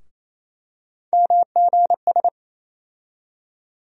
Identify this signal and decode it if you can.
Morse code, heard as a keyed tone, about 28 wpm, 700 Hz. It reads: MGS
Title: USA Call Sign Postfixes - 3 Letters 28wpm